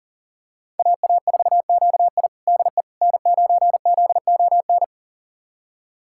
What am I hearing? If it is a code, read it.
Morse code, heard as a keyed tone, about 40 wpm, 700 Hz. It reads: AA4QI DE N9ZON
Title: USA Callsign DE Callsign - 5 Characters - 40wpm